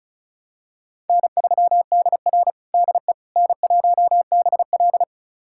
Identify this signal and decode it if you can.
Morse code, heard as a keyed tone, about 34 wpm, 700 Hz. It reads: N3DR DE N1BL
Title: USA Callsign DE Callsign - 4 Characters - 35wpm